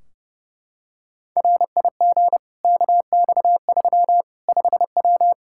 Morse code, heard as a keyed tone, about 30 wpm, 700 Hz. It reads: RIG KX3 5W
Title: Practice QSO - Exchange 2 - Info - 30wpm